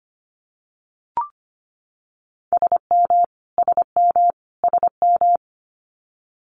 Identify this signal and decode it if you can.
Morse code, heard as a keyed tone, about 25 wpm, 700 Hz. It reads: SM SM SM
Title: Multi-letter Phonemes Repeated 3x 25wpm